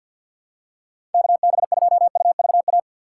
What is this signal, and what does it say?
Morse code, heard as a keyed tone, about 52 wpm, 700 Hz. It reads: KB2UVA
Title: USA Call Signs - 6 Characters - 4x Character Spacing - 50wpm